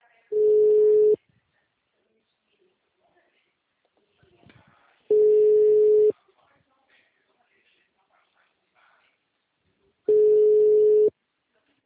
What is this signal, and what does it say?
Ringback tone (Europe, 425 Hz), with speech.
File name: out- -2010- - - .95391